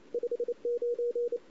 Morse code, heard as a keyed tone, about 29 wpm, 480 Hz. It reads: 59
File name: CW / 2008-ss-cw-k5tr-op-n5rz -02:00